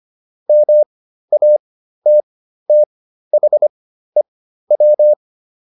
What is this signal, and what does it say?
Morse code, heard as a keyed tone, about 25 wpm, spaced as for 13 wpm, 600 Hz. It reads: MATTHEW